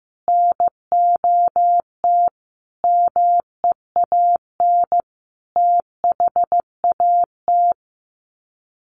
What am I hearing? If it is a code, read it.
Morse code, heard as a keyed tone, about 15 wpm, 700 Hz. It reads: NOT MEAN THAT
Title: Sets of 5 Words Encore2 15wpm